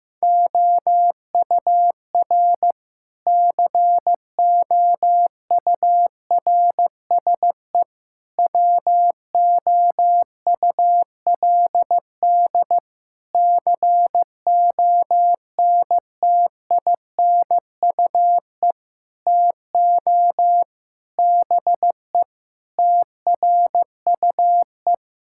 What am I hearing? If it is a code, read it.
Morse code, heard as a keyed tone, about 15 wpm, 700 Hz. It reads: OUR COURSE WOULD CONTINUE TO BE TRUE
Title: Sentences from Top 1000 Words - Review Part 6 - 15wpm